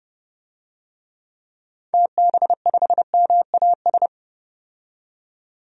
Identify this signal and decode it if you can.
Morse code, heard as a keyed tone, about 30 wpm, 700 Hz. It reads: TB5MAS
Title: DXCC-100 Call Signs - 6 Characters - 30wpm